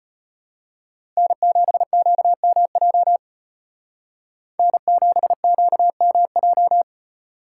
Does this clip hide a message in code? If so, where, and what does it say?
Yes: Morse code from 1.2 to 6.8 s, heard as a keyed tone, about 35 wpm, 700 Hz. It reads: N7QMJ N7QMJ